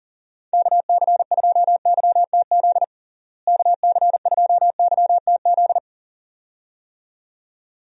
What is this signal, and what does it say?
Morse code, heard as a keyed tone, about 37 wpm, 700 Hz. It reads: KC2YTZ KC2YTZ
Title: USA Call Signs - 6 Characters - Repeated 2x - 40wpm